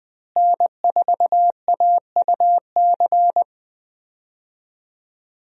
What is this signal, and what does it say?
Morse code, heard as a keyed tone, about 20 wpm, 700 Hz. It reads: N4AUC
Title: USA Call Signs - 5 Characters - Repeated 2x - 20wpm